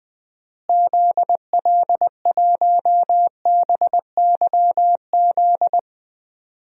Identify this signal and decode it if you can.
Morse code, heard as a keyed tone, about 21 wpm, 700 Hz. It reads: ZL1BYZ
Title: DXCC-100 Call Signs - 6 Characters - Repeated 2x - 20wpm